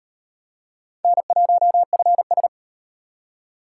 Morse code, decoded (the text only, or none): N1FS